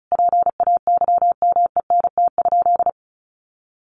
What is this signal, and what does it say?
Morse code, heard as a keyed tone, about 34 wpm, 700 Hz. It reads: PAYMENT?
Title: Sentences from Top 1400 Words 35wpm